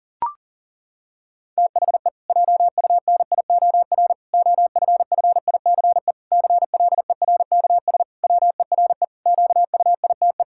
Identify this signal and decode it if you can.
Morse code, heard as a keyed tone, about 40 wpm, 700 Hz. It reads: THE JUNIOR OFFICE CLERKS WERE QUITE
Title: Pangram Challenge v2 - 40wpm